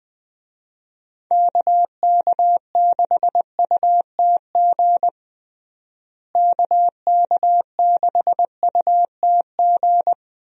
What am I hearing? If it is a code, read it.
Morse code, heard as a keyed tone, about 20 wpm, 700 Hz. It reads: KK6UTG KK6UTG
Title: USA Call Signs - 6 Characters - Repeated 3x - 20wpm